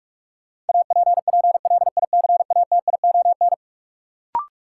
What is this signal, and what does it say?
Morse code, heard as a keyed tone, about 46 wpm, 700 Hz. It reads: APPLICATION
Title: Sentences from Top 1200 Words - New Word Drill Alt - 45wpm